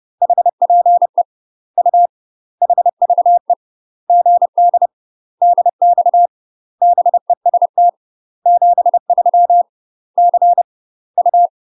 Morse code, heard as a keyed tone, about 30 wpm, 700 Hz. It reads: HPE U HVE GD DX BEST 73 C U
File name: UK MASTER QSOs TRAINING / 30WPM / UK MASTER QSOs 0008